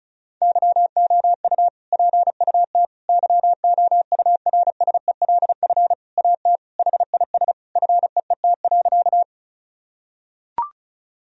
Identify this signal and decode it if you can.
Morse code, heard as a keyed tone, about 36 wpm, 700 Hz. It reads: YOU PUT YOURSELF AT HIS FEET.
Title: Sentences from Top 1000 Words - Review Part 1 - 35wpm